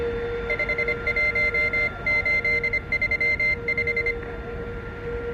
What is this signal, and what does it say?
Morse code, heard as a keyed tone, about 25 wpm, 2100 Hz. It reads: 51835